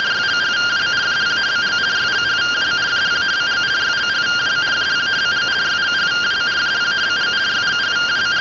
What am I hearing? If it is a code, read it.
Two-tone teleprinter signal, about 170 Hz shift.